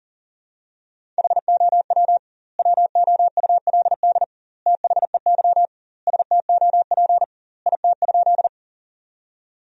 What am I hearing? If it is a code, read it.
Morse code, heard as a keyed tone, about 40 wpm, 700 Hz. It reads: HOW WOULD THEY STOP IT?
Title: 2x Word-Spacing - Sentences from Top 300 Words - 40wpm